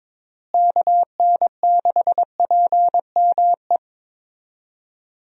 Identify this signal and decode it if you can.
Morse code, heard as a keyed tone, about 22 wpm, 700 Hz. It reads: KN6PME
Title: USA Call Signs - 6 Characters - Repeated 2x - 22wpm